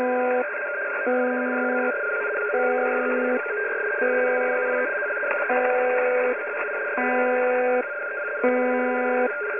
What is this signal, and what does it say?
Two-tone teleprinter signal, about 170 Hz shift.
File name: uploader / websdr recording 201